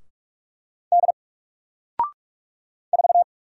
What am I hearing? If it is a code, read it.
Morse code, heard as a keyed tone, about 48 wpm, 700 Hz. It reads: D 4
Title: Single Character (Letter:Number:Punctuation:Prosign) Alt 45wpm